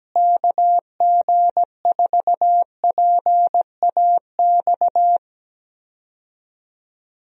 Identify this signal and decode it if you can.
Morse code, heard as a keyed tone, about 17 wpm, 700 Hz. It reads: KG4PAX
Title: USA Call Signs - 6 Characters - Repeated 2x - 17wpm